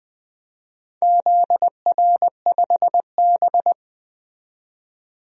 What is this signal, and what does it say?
Morse code, heard as a keyed tone, about 21 wpm, 700 Hz. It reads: ZR5B
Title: DXCC-100 Call Signs - 4 Characters - 20wpm